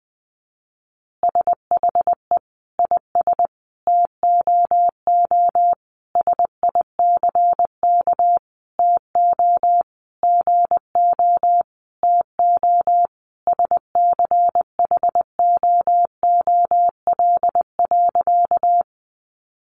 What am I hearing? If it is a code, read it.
Morse code, heard as a keyed tone, about 20 wpm, 700 Hz. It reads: SHE IS TOO SICK TO GO TO SCHOOL.
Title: Easy Phrases 20wpm